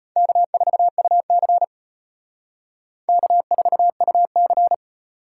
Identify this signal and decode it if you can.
Morse code, heard as a keyed tone, about 32 wpm, 700 Hz. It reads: K4UC K4UC